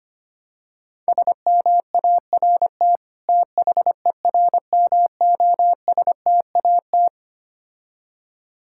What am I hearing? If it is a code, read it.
Morse code, heard as a keyed tone, about 25 wpm, 700 Hz. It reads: SMART THERMOSTAT
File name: practice / 25wpm / Difficult Words with Hints - 25wpm